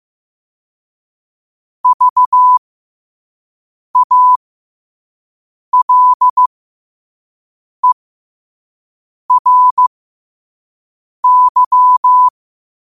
Morse code, heard as a keyed tone, about 15 wpm, spaced as for 5 wpm, 1000 Hz. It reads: VALERY
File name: qotdfiles / 5wpm